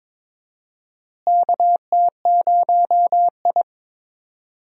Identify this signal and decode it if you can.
Morse code, heard as a keyed tone, about 22 wpm, 700 Hz. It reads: KT0I
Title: USA Call Signs - 4 Characters - Repeated 2x - 22wpm